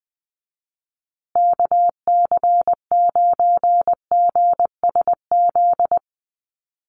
Morse code, heard as a keyed tone, about 20 wpm, 700 Hz. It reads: KC9GSZ
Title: USA Call Signs Repeated 3x 20wpm